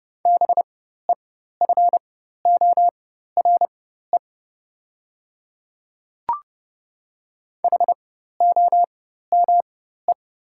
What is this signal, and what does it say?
Morse code, heard as a keyed tone, about 30 wpm, spaced as for 14 wpm, 690 Hz. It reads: BEFORE HOME
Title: 4x Character Spacing - Top 200 Words - 30wpm